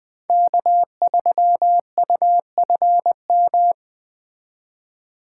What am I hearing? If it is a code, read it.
Morse code, heard as a keyed tone, about 20 wpm, 700 Hz. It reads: K3UFM
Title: Speed Racing - USA Call Signs 17wpm